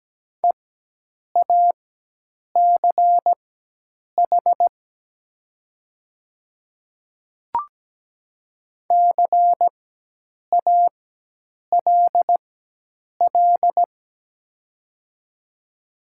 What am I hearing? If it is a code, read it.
Morse code, heard as a keyed tone, about 17 wpm, spaced as for 8 wpm, 700 Hz. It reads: EACH CALL
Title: 4x Character Spacing - Top 200 Words - 17wpm